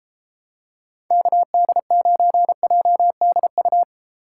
Morse code, heard as a keyed tone, about 31 wpm, 700 Hz. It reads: KD9JDU